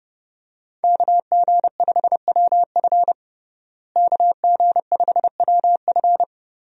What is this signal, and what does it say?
Morse code, heard as a keyed tone, about 30 wpm, 700 Hz. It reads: KG5WF KG5WF
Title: USA Call Signs - 5 Characters - Repeated 3x - 30wpm